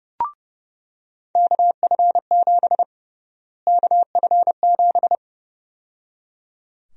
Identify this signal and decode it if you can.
Morse code, heard as a keyed tone, about 32 wpm, 700 Hz. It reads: KF7 KF7
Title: USA Call Sign Prefixes - Repeated 2x - 30wpm